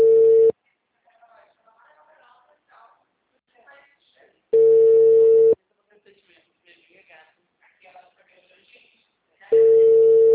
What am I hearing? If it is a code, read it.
Ringback tone (China, 450 Hz), with speech.